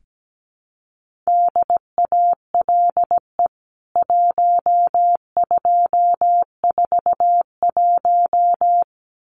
Morse code, heard as a keyed tone, about 17 wpm, 700 Hz. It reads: DALE 1241
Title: CWT Exchange 17wpm